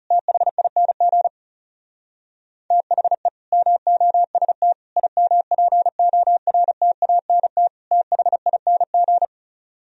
Morse code, heard as a keyed tone, about 38 wpm, 700 Hz. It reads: THING THE MOST IMPORTANT THING
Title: Speed-Racing - Sets of 4 Words 30wpm